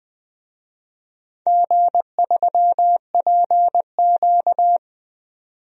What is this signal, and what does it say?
Morse code, heard as a keyed tone, about 21 wpm, 700 Hz. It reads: G3PQ
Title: DXCC-100 Call Signs - 4 Characters - 20wpm